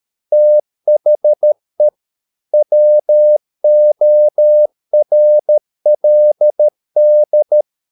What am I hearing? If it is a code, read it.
Morse code, heard as a keyed tone, about 13 wpm, 600 Hz. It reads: THE WORLD